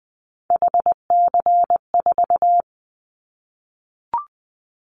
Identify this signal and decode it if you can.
Morse code, heard as a keyed tone, about 20 wpm, 700 Hz. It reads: HC4
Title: Callsign Prefixes for Top 100 DXCC - 20wpm